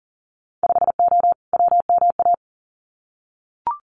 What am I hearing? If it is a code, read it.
Morse code, heard as a keyed tone, about 39 wpm, 700 Hz. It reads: 5O WMA
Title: Winter Field Day Exchanges Encore 40wpm